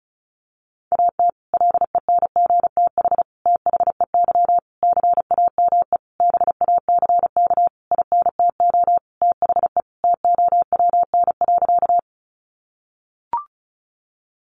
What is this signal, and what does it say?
Morse code, heard as a keyed tone, about 34 wpm, 700 Hz. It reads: AT LENGTH THEY CAME BACK INTO THE TOWN.
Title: Sentences from Top 1000 Words 35wpm